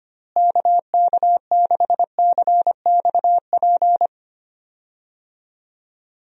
Morse code, heard as a keyed tone, about 25 wpm, 700 Hz. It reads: KK6CXP
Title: USA Call Signs - 6 Characters - Repeated 2x - 25wpm